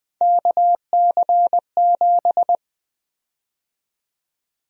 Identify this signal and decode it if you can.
Morse code, heard as a keyed tone, about 20 wpm, 700 Hz. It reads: KC7
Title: USA Call Sign Prefixes - 20wpm